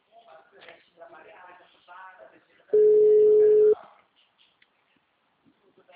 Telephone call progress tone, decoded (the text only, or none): ringback tone